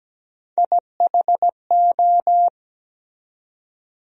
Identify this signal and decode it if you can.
Morse code, heard as a keyed tone, about 17 wpm, 700 Hz. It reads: IHO
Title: USA Call Sign Postfixes - 3 Letters 17wpm